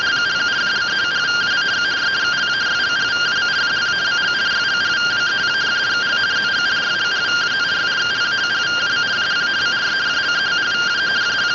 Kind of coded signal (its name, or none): two-tone teleprinter signal